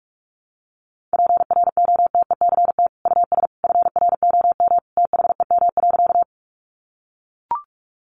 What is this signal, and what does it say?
Morse code, heard as a keyed tone, about 44 wpm, 700 Hz. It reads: PROTECT US FROM THEM.
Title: Sentences from Top 900 Words 45wpm